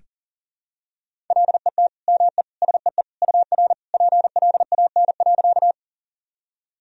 Morse code, heard as a keyed tone, about 40 wpm, 700 Hz. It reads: LET ME SEE UR PLAN.
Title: Sentences from Top 500 Words and Abbreviations 40wpm